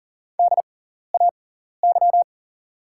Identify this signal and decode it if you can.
Morse code, heard as a keyed tone, about 40 wpm, spaced as for 14 wpm, 700 Hz. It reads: DAY
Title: 6x Character Spacing - Top 100 Words - 40wpm